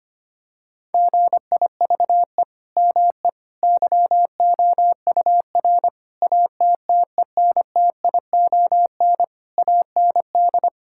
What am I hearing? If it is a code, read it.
Morse code, heard as a keyed tone, about 25 wpm, 700 Hz. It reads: GIVE ME YOUR ATTENTION AND
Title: Sentences from Top 1000 Words - Review Part 6 - 25wpm